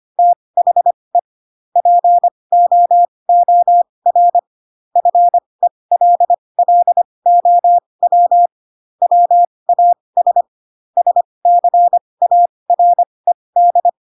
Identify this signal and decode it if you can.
Morse code, heard as a keyed tone, about 25 wpm, spaced as for 20 wpm, 700 Hz. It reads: THE POOR FELLOW WAS SCARED
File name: Pinocchio / Pinocchio ch1 15 25 ik1hge